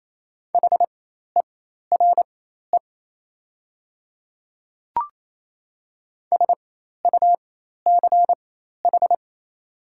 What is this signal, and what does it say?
Morse code, heard as a keyed tone, about 28 wpm, spaced as for 13 wpm, 700 Hz. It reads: HERE SUCH
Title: 4x Character Spacing - Top 500 Words - 28wpm